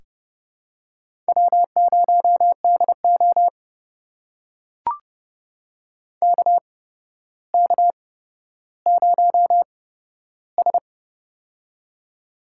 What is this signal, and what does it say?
Morse code, heard as a keyed tone, about 32 wpm, spaced as for 13 wpm, 700 Hz. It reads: W0DO KK0S
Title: USA Call Signs - 4 Characters - 8x Character Spacing - 30wpm